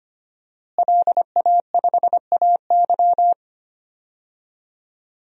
Morse code, heard as a keyed tone, about 25 wpm, 700 Hz. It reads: LA5AY
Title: DXCC-100 Call Signs - 5 Characters - Repeated 2x - 25wpm